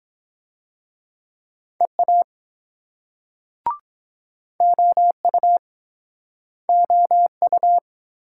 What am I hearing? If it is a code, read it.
Morse code, heard as a keyed tone, about 22 wpm, 700 Hz. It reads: EA OU OU